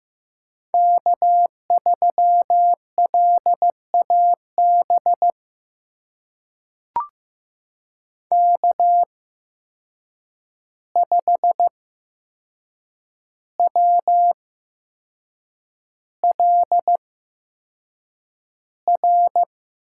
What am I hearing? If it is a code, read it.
Morse code, heard as a keyed tone, about 15 wpm, spaced as for 6 wpm, 700 Hz. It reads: K3LAB K5WLR
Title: USA Call Signs - 5 Characters - 8x Character Spacing - 15wpm